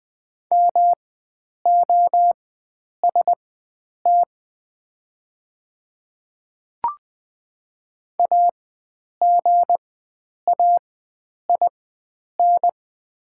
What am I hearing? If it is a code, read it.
Morse code, heard as a keyed tone, about 20 wpm, spaced as for 9 wpm, 700 Hz. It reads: MOST AGAIN
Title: 4x Character Spacing - Top 300 Words - 20wpm